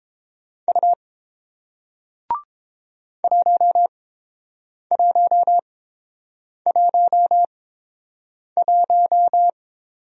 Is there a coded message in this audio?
Yes: Morse code from 0.7 to 9.5 s, heard as a keyed tone, about 30 wpm, spaced as for 7 wpm, 700 Hz. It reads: U 1111